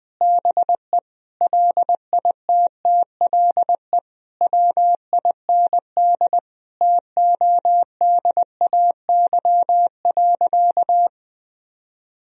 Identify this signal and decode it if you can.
Morse code, heard as a keyed tone, about 21 wpm, 700 Hz. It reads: BE LITTLE WIND TODAY.
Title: Sentences from Top 1000 Words - Review Part 2 - 20wpm